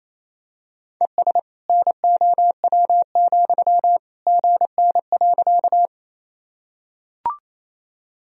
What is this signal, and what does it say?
Morse code, heard as a keyed tone, about 28 wpm, 700 Hz. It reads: ES NOW, GN.
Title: Sentences from Top 300 Words and Abbreviations 28wpm